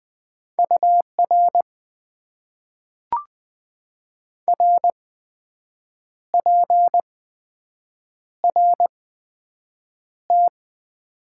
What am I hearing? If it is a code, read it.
Morse code, heard as a keyed tone, about 20 wpm, spaced as for 7 wpm, 700 Hz. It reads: UR RPRT
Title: Top 90 QSO Elements - 8x Character Spacing - 20wpm